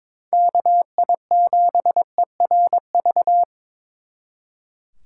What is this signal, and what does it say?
Morse code, heard as a keyed tone, about 26 wpm, 700 Hz. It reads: KI7ERV
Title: Speed Racing - USA 2x3 Call Signs 22wpm